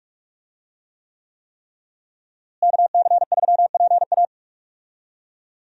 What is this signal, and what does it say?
Morse code, heard as a keyed tone, about 46 wpm, 700 Hz. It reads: KC3PA
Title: USA Call Signs - 5 Characters - Repeated 3x - 45wpm